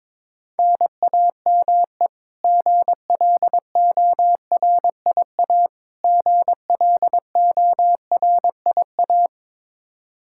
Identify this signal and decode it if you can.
Morse code, heard as a keyed tone, about 22 wpm, 700 Hz. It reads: NAME GLORIA GLORIA
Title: K1USN SST QSO Practice 22wpm